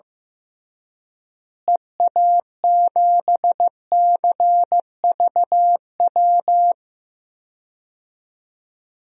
Morse code, heard as a keyed tone, about 15 wpm, 700 Hz. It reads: EA7CVW
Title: DXCC-100 Call Signs - 6 Characters - Repeated 3x - 15wpm